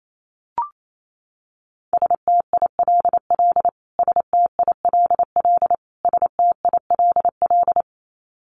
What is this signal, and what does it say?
Morse code, heard as a keyed tone, about 28 wpm, 700 Hz. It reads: STILL STILL STILL
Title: Top 500 Words Repeated 28wpm